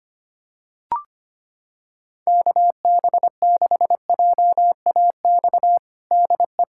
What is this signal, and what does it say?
Morse code, heard as a keyed tone, about 25 wpm, 700 Hz. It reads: KB6JAX DE
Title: USA Callsign DE Callsign - 6 Characters - 25wpm